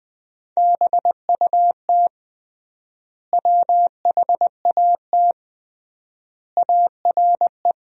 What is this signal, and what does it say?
Morse code, heard as a keyed tone, about 21 wpm, 700 Hz. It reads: BUT WHAT ARE
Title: 3x Word-Spacing - Sentences from Top 100 Words - 20wpm